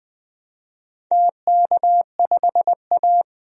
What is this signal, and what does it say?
Morse code, heard as a keyed tone, about 21 wpm, 700 Hz. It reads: TK5A
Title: DXCC-100 Call Signs - 4 Characters - 20wpm